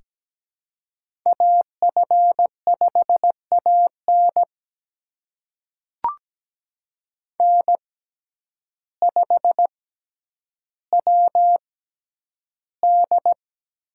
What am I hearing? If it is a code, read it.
Morse code, heard as a keyed tone, about 17 wpm, spaced as for 9 wpm, 700 Hz. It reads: AF5AN N5WD
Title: USA Call Signs - 5 Characters - 6x Character Spacing - 17wpm